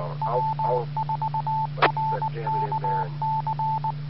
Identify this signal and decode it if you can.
Morse code, heard as a keyed tone, about 19 wpm, 880 Hz, with radio noise. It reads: W4RKC
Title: Bluemont, VA Skywarn Repeater - WA4TSC